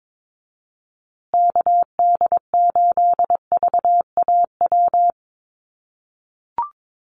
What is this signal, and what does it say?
Morse code, heard as a keyed tone, about 22 wpm, 700 Hz. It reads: KD8VAW
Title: USA Call Signs Repeated 3x 22wpm